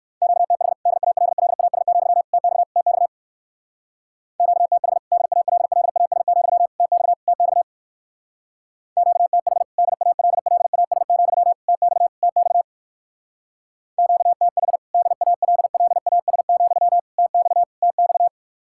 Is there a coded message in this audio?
Yes: Morse code from 0.2 to 18.3 s, heard as a keyed tone, about 60 wpm, 700 Hz. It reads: QTH DALLAS, TX TX QTH DALLAS, TX TX QTH DALLAS, TX TX QTH DALLAS, TX TX